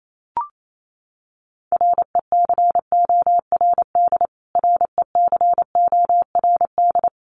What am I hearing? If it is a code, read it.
Morse code, heard as a keyed tone, about 28 wpm, 700 Hz. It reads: RECORD RECORD
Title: Top 500 Words Repeated 28wpm